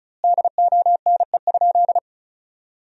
Morse code, heard as a keyed tone, about 34 wpm, 700 Hz. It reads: DONE?
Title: 2x Word-Spacing - Sentences from Top 500 Words - 35wpm